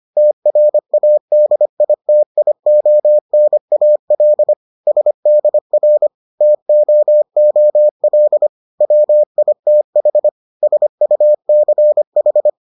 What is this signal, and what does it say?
Morse code, heard as a keyed tone, about 25 wpm, 600 Hz. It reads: TRADITIONAL SDR TOOL WITH SUCH